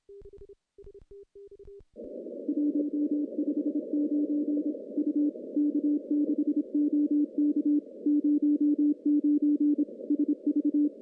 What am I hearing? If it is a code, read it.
Morse code, heard as a keyed tone, about 27 wpm, 290 Hz. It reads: 159 U K6OK 09 SV
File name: CW-SS-k5tr